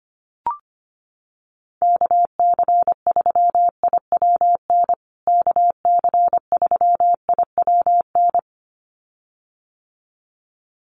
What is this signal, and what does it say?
Morse code, heard as a keyed tone, about 25 wpm, 700 Hz. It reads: KC3IWN KC3IWN
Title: USA Call Signs Repeated 2x 25wpm